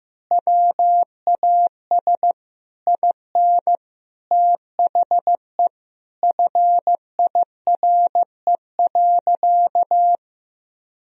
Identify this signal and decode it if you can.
Morse code, heard as a keyed tone, about 16 wpm, 700 Hz. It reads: WAS IN THE FIRE.
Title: Sentences from Top 1000 Words - Review Part 4 - 15wpm